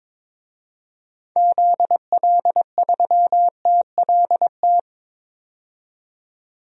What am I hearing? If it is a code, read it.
Morse code, heard as a keyed tone, about 22 wpm, 700 Hz. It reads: ZL3TLT
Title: DXCC-100 Call Signs - 6 Characters - 22wpm